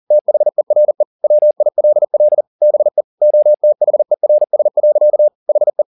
Morse code, heard as a keyed tone, about 40 wpm, 600 Hz. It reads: THERE WILL BE OTHERS. HE
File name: the old man and the sea 2 40wpm